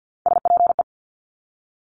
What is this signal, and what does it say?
Morse code, heard as a keyed tone, about 50 wpm, 700 Hz. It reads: SPE
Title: Top Three Letter Combinations in Words 50wpm